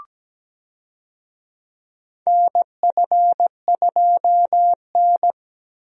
Morse code, heard as a keyed tone, about 17 wpm, 700 Hz. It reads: NF2N
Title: USA Call Signs - 4 Characters - Repeated 3x - 17wpm